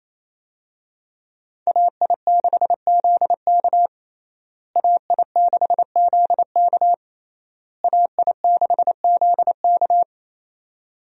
Morse code, heard as a keyed tone, about 28 wpm, 700 Hz. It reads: AI6ZK AI6ZK AI6ZK
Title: USA Call Signs - 5 Characters - Repeated 3x - 28wpm